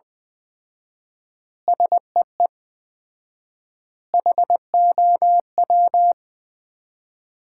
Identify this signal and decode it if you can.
Morse code, heard as a keyed tone, about 20 wpm, 700 Hz. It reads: SEE HOW
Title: 4x Word-Spacing - Sets of 5 Words - 20wpm